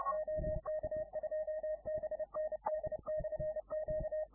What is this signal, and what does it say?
Morse code, heard as a keyed tone, about 29 wpm, 620 Hz. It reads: OK2BND/Q